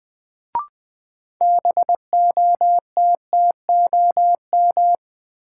Morse code, heard as a keyed tone, about 20 wpm, 700 Hz. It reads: BOTTOM
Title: Sets of 3 Words - New Word Drill - 20wpm